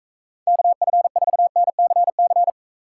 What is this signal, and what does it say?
Morse code, heard as a keyed tone, about 45 wpm, 700 Hz. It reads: KF4NCC